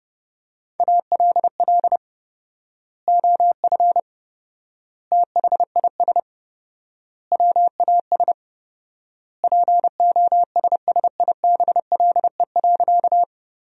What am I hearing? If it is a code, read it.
Morse code, heard as a keyed tone, about 30 wpm, 700 Hz. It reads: ALL OF THIS WAS POSSIBLE.
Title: 4x Word-Spacing - Sentences from Top 500 Words - 30wpm